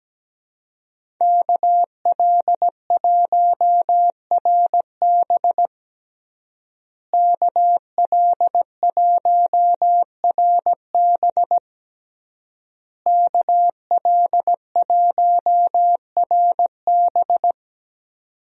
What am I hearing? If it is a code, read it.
Morse code, heard as a keyed tone, about 17 wpm, 700 Hz. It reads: KL1RB KL1RB KL1RB
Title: USA Call Signs - 5 Characters - Repeated 3x - 17wpm